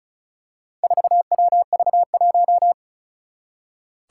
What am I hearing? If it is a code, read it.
Morse code, heard as a keyed tone, about 36 wpm, 700 Hz. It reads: 4WV1
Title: Four Challenging Letter-Number Combinations 35wpm